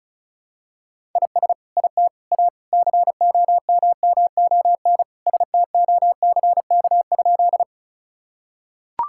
Morse code, heard as a keyed tone, about 34 wpm, 700 Hz. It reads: IS IT A COMMON STOCK?
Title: Sentences from Top 1000 Words - Review Part 5 - 35wpm